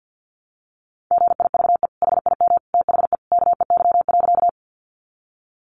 Morse code, heard as a keyed tone, about 50 wpm, 700 Hz. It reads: GIVE HIM THE KEY.
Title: Sentences from Top 600 Words 50wpm